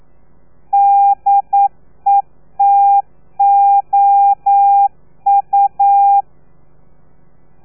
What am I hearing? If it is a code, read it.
Morse code, heard as a keyed tone, about 9 wpm, 800 Hz. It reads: DETOU
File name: telegrafia / i2tfj / Lez3 40